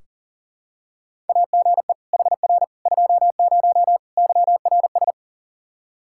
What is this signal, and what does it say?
Morse code, heard as a keyed tone, about 40 wpm, 700 Hz. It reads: AGE HR 20 YRS
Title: Practice QSO - Exchange 3 - More Info - 40wpm